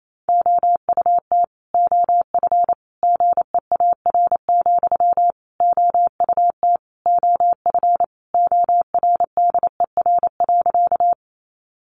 Morse code, heard as a keyed tone, about 28 wpm, 700 Hz. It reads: OUT OF GEAR, OUT OF ORDER.
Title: Sentences from Top 1400 Words 28wpm